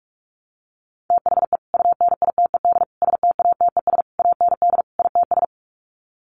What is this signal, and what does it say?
Morse code, heard as a keyed tone, about 44 wpm, 700 Hz. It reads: THE UNITED STATES AND ITS
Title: Sets of 5 Words Encore 45wpm